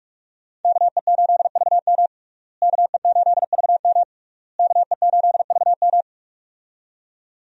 Morse code, heard as a keyed tone, about 41 wpm, 700 Hz. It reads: KE8VM KE8VM KE8VM
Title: USA Call Signs - 5 Characters - Repeated 3x - 45wpm